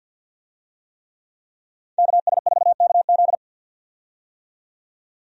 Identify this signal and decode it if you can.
Morse code, heard as a keyed tone, about 47 wpm, 700 Hz. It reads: KS4KZ
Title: USA Call Signs - 5 Characters - 50wpm